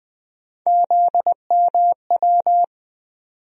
Morse code, heard as a keyed tone, about 20 wpm, 700 Hz. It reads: ZMW
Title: USA Call Sign Postfixes - 3 Letters 20wpm